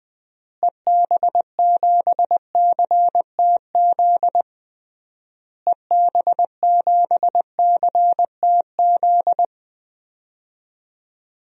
Morse code, heard as a keyed tone, about 20 wpm, 700 Hz. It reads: EB7CTZ EB7CTZ
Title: DXCC-100 Call Signs - 6 Characters - Repeated 3x - 20wpm